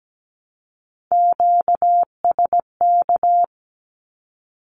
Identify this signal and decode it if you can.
Morse code, heard as a keyed tone, about 17 wpm, 700 Hz. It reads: QSK
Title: Q-Codes Repeated 3x 17wpm